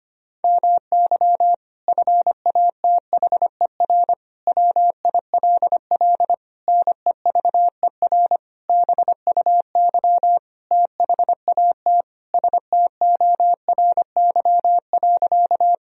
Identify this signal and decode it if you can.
Morse code, heard as a keyed tone, about 25 wpm, 690 Hz. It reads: MY FATHER WILL NEVER BUY THAT STORY.
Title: Sentences from Top 1000 Words - Review Part 4 - 25wpm